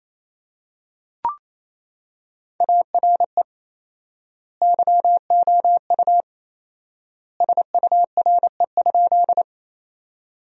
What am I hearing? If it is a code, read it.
Morse code, heard as a keyed tone, about 28 wpm, 700 Hz. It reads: ARE YOU SURE?
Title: 4x Word-Spacing - Sentences from Top 300 Words - 28wpm